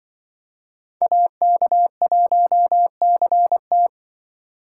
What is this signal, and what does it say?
Morse code, heard as a keyed tone, about 24 wpm, 700 Hz. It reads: AK1CT
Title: Speed Racing - USA Call Signs 20wpm